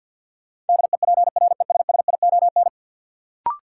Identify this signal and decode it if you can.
Morse code, heard as a keyed tone, about 49 wpm, 700 Hz. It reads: DEPRESSION
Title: Sentences from Top 1300 Words - New Word Drill Alt - 50wpm